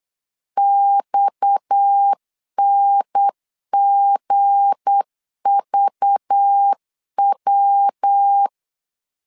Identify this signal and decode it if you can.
Morse code, heard as a keyed tone, about 8 wpm, 800 Hz. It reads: XNGVW
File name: LEZIONI / 7 GF9Z / 7 G 8wpm 3118